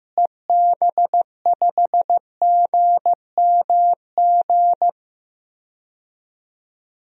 Morse code, heard as a keyed tone, about 15 wpm, 700 Hz. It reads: EB5GMG
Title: DXCC-100 Call Signs - 6 Characters - Repeated 3x - 15wpm